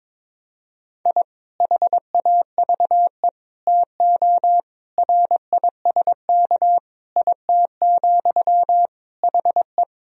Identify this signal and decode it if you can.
Morse code, heard as a keyed tone, about 22 wpm, 700 Hz. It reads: I HAVE TO RISK IT, HE
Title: Sentences from Top 1000 Words - Review Part 4 - 22wpm